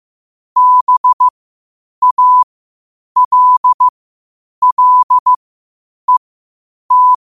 Morse code, heard as a keyed tone, about 15 wpm, spaced as for 8 wpm, 1000 Hz. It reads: BALLET
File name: qotdfiles / 8wpm